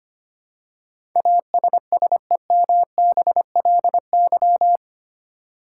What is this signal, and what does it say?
Morse code, heard as a keyed tone, about 25 wpm, 700 Hz. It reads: ASSEMBLY
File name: Eight Letter Words - 25wpm